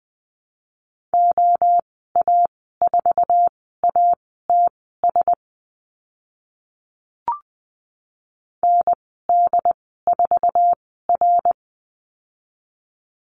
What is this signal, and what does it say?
Morse code, heard as a keyed tone, about 20 wpm, spaced as for 15 wpm, 700 Hz. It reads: OA4ATS ND4R
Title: 069 Lesson 12g - Letter d - Call Signs 20wpm